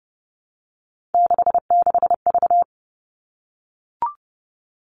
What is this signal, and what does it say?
Morse code, heard as a keyed tone, about 30 wpm, 700 Hz. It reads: 66V
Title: All Cut Numbers - 3-Digit 30wpm